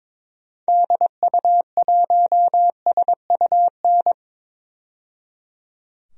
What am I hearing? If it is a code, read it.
Morse code, heard as a keyed tone, about 22 wpm, 700 Hz. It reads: DU1SUN
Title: DXCC-100 Call Signs - 6 Characters - 22wpm